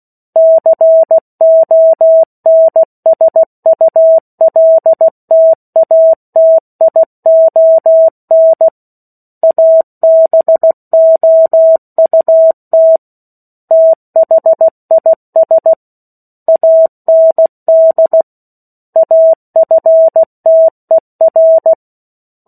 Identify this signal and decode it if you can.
Morse code, heard as a keyed tone, about 16 wpm, 650 Hz. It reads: CONSULTATION ABOUT THIS AND AFTER
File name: 03-Chapter-03 16 WPM